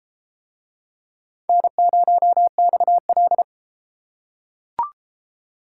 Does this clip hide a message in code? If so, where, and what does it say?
Yes: Morse code from 1.5 to 3.4 s, heard as a keyed tone, about 30 wpm, 700 Hz. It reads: N0XL